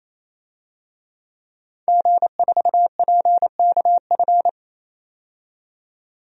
Morse code, heard as a keyed tone, about 28 wpm, 700 Hz. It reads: G4PKF
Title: DXCC-100 Call Signs - 5 Characters - 28wpm